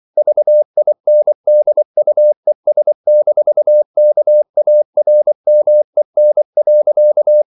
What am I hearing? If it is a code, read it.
Morse code, heard as a keyed tone, about 24 wpm, 600 Hz. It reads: VINDUES-KARMEN.
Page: Ferie 24 wpm